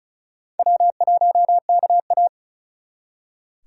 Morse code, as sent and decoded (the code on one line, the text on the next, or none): .-- .---- -.- .-
W1KA